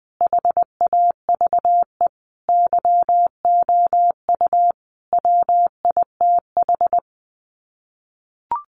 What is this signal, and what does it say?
Morse code, heard as a keyed tone, about 20 wpm, 700 Hz. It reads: HAVE YOU WITH
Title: Sets of 5 Words Encore2 20wpm